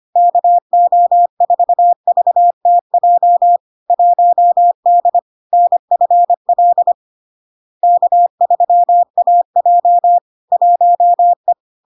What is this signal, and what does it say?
Morse code, heard as a keyed tone, about 25 wpm, 700 Hz. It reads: KO4VTJ 1D NFL K3AJ 1E
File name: LARK-FD-2022-Log-in-CW-25wpm-700-Hz